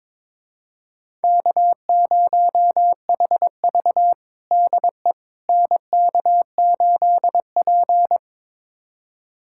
Morse code, heard as a keyed tone, about 22 wpm, 700 Hz. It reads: K0HV DE NK8P
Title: USA Callsign DE Callsign - 4 Characters - 22wpm